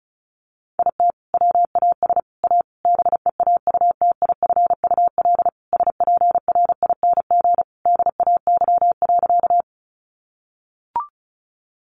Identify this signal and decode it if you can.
Morse code, heard as a keyed tone, about 34 wpm, 700 Hz. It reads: IT WAS A BEAUTIFUL SPRING DAY.
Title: Sentences from Top 1200 Words 35wpm